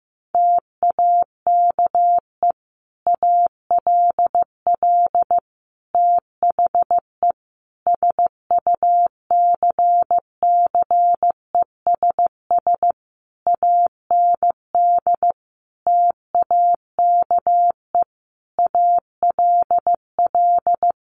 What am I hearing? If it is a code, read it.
Morse code, heard as a keyed tone, about 14 wpm, 700 Hz. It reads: TAKE ALL THE SUCCESS AND TAKE ALL
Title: Sentences from Top 900 Words 15wpm